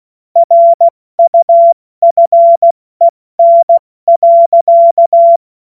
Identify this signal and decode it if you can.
Morse code, heard as a keyed tone, about 16 wpm, 680 Hz. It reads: RUFEN.
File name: Morsekurs / Witze / Witz 06. September 2022 70 F80